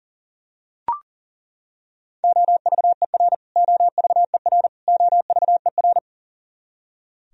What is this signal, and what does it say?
Morse code, heard as a keyed tone, about 40 wpm, 700 Hz. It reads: OVER OVER OVER
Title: Top 100 Words Repeated 40wpm